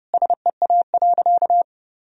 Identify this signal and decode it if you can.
Morse code, heard as a keyed tone, about 30 wpm, 700 Hz. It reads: SEA.
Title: Sentences from Top 1000 Words - Review Part 2 - 30wpm